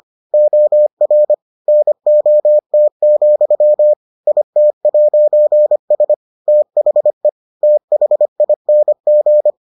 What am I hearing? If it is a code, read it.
Morse code, heard as a keyed tone, about 25 wpm, 600 Hz. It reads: OR NOT, IT'S THE THING